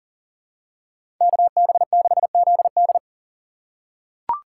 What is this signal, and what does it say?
Morse code, heard as a keyed tone, about 40 wpm, 700 Hz. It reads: KB6ZD
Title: USA Call Signs - 3x Character Spacing - 40wpm